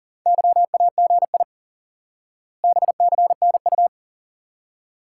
Morse code, heard as a keyed tone, about 40 wpm, 700 Hz. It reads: YAGI BCNU
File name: IQE Course 3 - Lesson 89 - YAGI - Two Elements - 40wpm